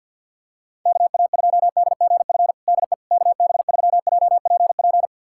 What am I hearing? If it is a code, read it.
Morse code, heard as a keyed tone, about 51 wpm, 700 Hz. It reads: KA2DGF DE KB3JPP
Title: USA Callsign DE Callsign - 6 Characters - 50wpm